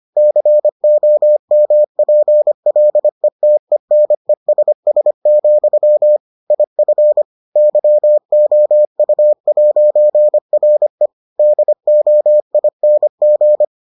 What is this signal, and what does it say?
Morse code, heard as a keyed tone, about 25 wpm, 600 Hz. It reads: COMPLETENESS, IF YOU'RE DOING